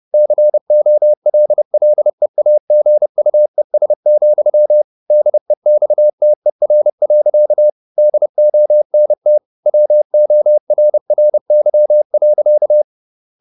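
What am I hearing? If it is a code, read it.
Morse code, heard as a keyed tone, about 30 wpm, 600 Hz. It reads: COLLEAGUES, DEXTER. DONT WORRY.
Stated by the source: Morse code at 30 wpm, the novel Never by Ken Follett in English, from CW med Gnister.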